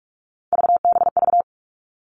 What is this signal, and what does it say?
Morse code, heard as a keyed tone, about 44 wpm, 700 Hz. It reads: VBV
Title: All Cut Numbers - 3-Digit 45wpm